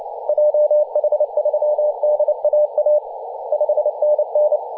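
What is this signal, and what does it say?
Morse code, heard as a keyed tone, about 30 wpm, 600 Hz, with radio noise. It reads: JH3DAA 5NN
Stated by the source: DX station in Papua New Guinea on 21.040 MHz in CW.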